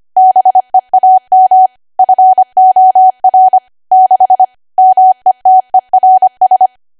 Morse code, heard as a keyed tone, about 25 wpm, 750 Hz. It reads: BEAM FOR 6 METERS
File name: Archive / 25 WPM / 25WPM